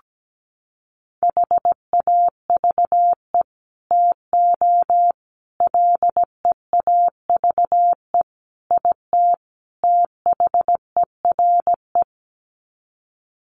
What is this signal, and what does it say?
Morse code, heard as a keyed tone, about 17 wpm, 700 Hz. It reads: HAVE TO LEAVE IT THERE
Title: Sets of 5 Words Encore2 17wpm